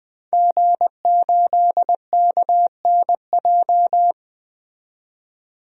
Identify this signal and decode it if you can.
Morse code, heard as a keyed tone, about 20 wpm, 700 Hz. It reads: G8KNJ
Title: DXCC-100 Call Signs - 5 Characters - 20wpm